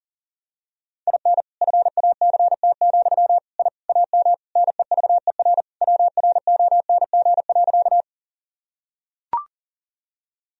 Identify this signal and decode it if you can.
Morse code, heard as a keyed tone, about 37 wpm, 700 Hz. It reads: IN FACT, I AM NEVER WRONG.
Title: Sentences from Top 1000 Words - Review Part 3 - 40wpm